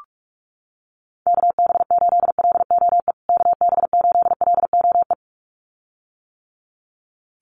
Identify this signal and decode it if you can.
Morse code, heard as a keyed tone, about 44 wpm, 700 Hz. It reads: KB8LOE KB8LOE
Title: USA Call Signs Repeated 2x 45wpm